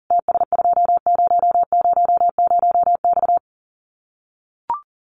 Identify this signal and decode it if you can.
Morse code, heard as a keyed tone, about 40 wpm, 700 Hz. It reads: TS2000X
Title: Common QSO Phrases 40wpm